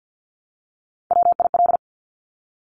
Morse code, heard as a keyed tone, about 50 wpm, 700 Hz. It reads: WIL
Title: Top Three Letter Combinations in Words 50wpm